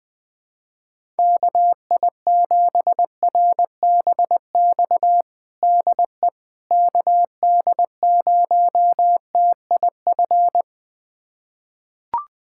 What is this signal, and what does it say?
Morse code, heard as a keyed tone, about 21 wpm, 700 Hz. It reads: KI7RBX DE KD0TIF
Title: USA Callsign DE Callsign - 6 Characters - 20wpm